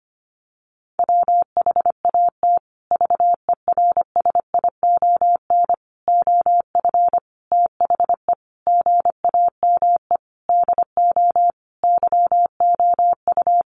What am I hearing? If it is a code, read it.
Morse code, heard as a keyed tone, about 25 wpm, 700 Hz. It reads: WHAT VERSION OF THE GAME DO YOU
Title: Sentences from Top 1400 Words 25wpm